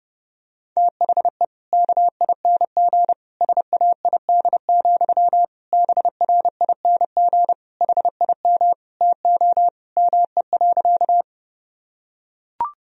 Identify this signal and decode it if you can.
Morse code, heard as a keyed tone, about 30 wpm, 700 Hz. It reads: THE KING SAID, BRING HIM TO ME.
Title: Sentences from Top 1000 Words - Review Part 3 - 30wpm